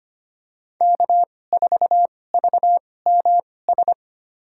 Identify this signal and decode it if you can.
Morse code, heard as a keyed tone, about 25 wpm, spaced as for 18 wpm, 700 Hz. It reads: K4VMS
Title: USA Call Signs - 5 Characters - 2x Character Spacing - 25wpm